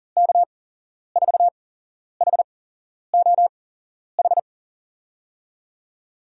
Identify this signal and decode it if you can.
Morse code, heard as a keyed tone, about 40 wpm, spaced as for 11 wpm, 700 Hz. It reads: K4HOH
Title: USA Call Signs - 6 Characters - 8x Character Spacing - 40wpm